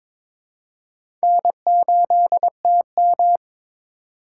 Morse code, heard as a keyed tone, about 22 wpm, 700 Hz. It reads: N8TM
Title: USA Call Signs - 2x Character Spacing - 22wpm